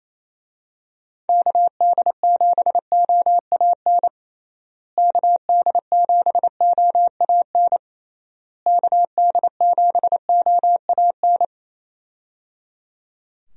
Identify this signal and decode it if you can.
Morse code, heard as a keyed tone, about 28 wpm, 700 Hz. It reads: KD7OAN KD7OAN KD7OAN
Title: USA Call Signs - 6 Characters - Repeated 3x - 28wpm